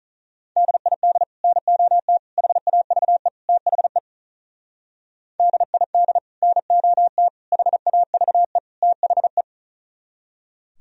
Morse code, heard as a keyed tone, about 47 wpm, 700 Hz. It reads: DID NOT HAVE THE DID NOT HAVE THE